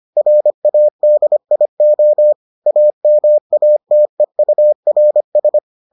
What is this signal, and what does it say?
Morse code, heard as a keyed tone, about 25 wpm, 600 Hz. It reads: RADIO AMATEURS